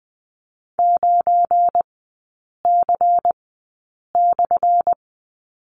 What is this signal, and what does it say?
Morse code, heard as a keyed tone, about 20 wpm, spaced as for 8 wpm, 700 Hz. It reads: 9C/
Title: 184 Lesson 30e - Letter b - Three Characters 20wpm